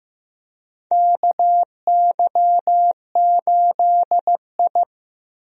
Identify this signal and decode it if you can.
Morse code, heard as a keyed tone, about 15 wpm, 700 Hz. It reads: KY8I
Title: USA Call Signs - 4 Characters - Repeated 3x - 15wpm